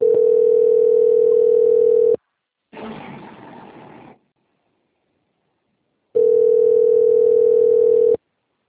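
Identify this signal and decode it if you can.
Ringback tone (North America), 440 + 480 Hz.
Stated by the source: Outbound phone call recorded on a PBX at Itatiba II.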